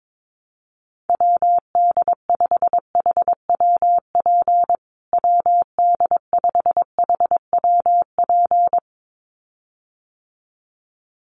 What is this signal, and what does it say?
Morse code, heard as a keyed tone, about 22 wpm, 700 Hz. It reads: WD5HWP WD5HWP
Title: USA Call Signs Repeated 2x 22wpm